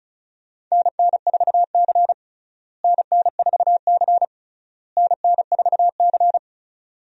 Morse code, heard as a keyed tone, about 36 wpm, 700 Hz. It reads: NN4C NN4C NN4C
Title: USA Call Signs - 4 Characters - Repeated 3x - 35wpm